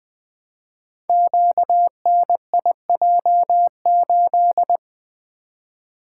Morse code, heard as a keyed tone, about 20 wpm, 700 Hz. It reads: QNIJ8
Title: Code Groups - 5 Letter:Numbers - 20wpm